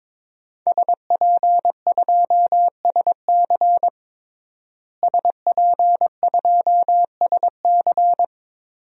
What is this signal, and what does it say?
Morse code, heard as a keyed tone, about 22 wpm, 700 Hz. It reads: SP2SC SP2SC
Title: DXCC-100 Call Signs - 5 Characters - Repeated 3x - 22wpm